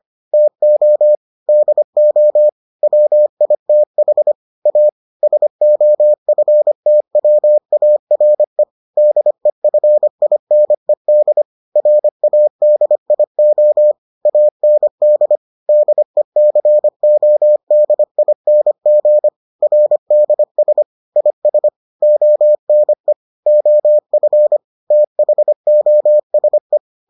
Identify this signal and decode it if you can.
Morse code, heard as a keyed tone, about 25 wpm, 600 Hz. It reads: TO DO WITH A SOFTWARE DEFINED RADIO AND DECODING RDS IS ONE OF THOSE